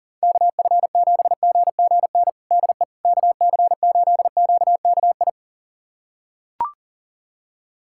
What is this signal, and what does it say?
Morse code, heard as a keyed tone, about 40 wpm, 700 Hz. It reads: KF7GGN DE KC8QKI
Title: USA Callsign DE Callsign 40wpm